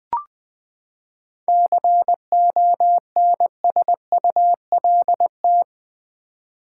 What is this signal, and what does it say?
Morse code, heard as a keyed tone, about 20 wpm, 700 Hz. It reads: CONSULT
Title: Seven Letter Words - 20wpm